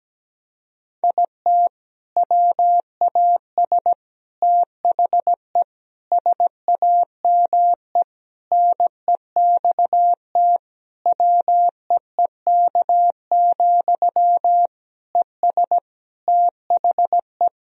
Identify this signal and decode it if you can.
Morse code, heard as a keyed tone, about 17 wpm, 700 Hz. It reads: IT WAS THE SAME NEXT WEEK, ES THE
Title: Sentences from Top 500 Words and Abbreviations 17wpm